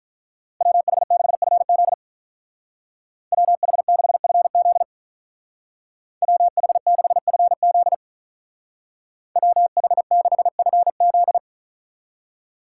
Morse code, heard as a keyed tone, about 43 wpm, 700 Hz. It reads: WH6FZ WH6FZ WH6FZ WH6FZ
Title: Speed Racing - USA 2x2 Call Signs 35wpm